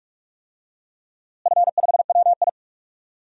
Morse code, heard as a keyed tone, about 47 wpm, 700 Hz. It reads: F5WI
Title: DXCC-100 Call Signs - 4 Characters - Repeated 3x - 45wpm